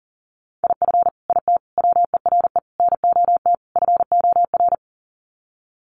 Morse code, heard as a keyed tone, about 40 wpm, 700 Hz. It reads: IF IT WERE NOT FOR
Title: Sets of 5 Words 40wpm